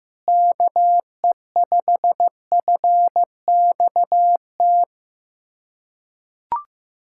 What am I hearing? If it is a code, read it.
Morse code, heard as a keyed tone, about 15 wpm, 700 Hz. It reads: KE5FXT
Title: USA Callsign DE Callsign - 6 Characters - 15wpm